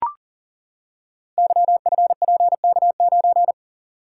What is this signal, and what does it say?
Morse code, heard as a keyed tone, about 42 wpm, 700 Hz. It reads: YFPK9
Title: Code Groups - 5 Letter:Numbers - 40wpm